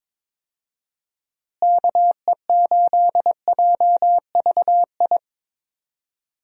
Morse code, heard as a keyed tone, about 22 wpm, 700 Hz. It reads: KE8JVI
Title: USA Call Signs - 6 Characters - Repeated 2x - 22wpm